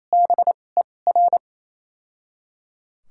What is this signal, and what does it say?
Morse code, heard as a keyed tone, about 28 wpm, spaced as for 20 wpm, 690 Hz. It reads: BER
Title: 2x Character Spacing - Top 200 Words - 28wpm